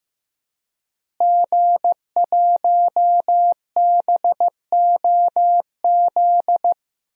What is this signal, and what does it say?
Morse code, heard as a keyed tone, about 15 wpm, 700 Hz. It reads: G1BOZ
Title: DXCC-100 Call Signs - 5 Characters - Repeated 3x - 15wpm